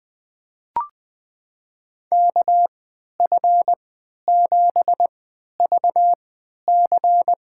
Morse code, heard as a keyed tone, about 20 wpm, spaced as for 11 wpm, 700 Hz. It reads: KF7VC
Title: USA Call Signs - 3x Character Spacing - 20wpm